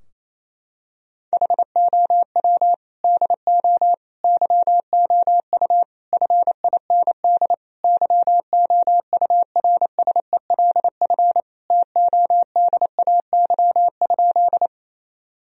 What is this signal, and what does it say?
Morse code, heard as a keyed tone, about 28 wpm, 700 Hz. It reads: HOW DO YOU FIND YOURSELF TODAY?
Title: Sentences from Top 1000 Words - Review Part 4 - 28wpm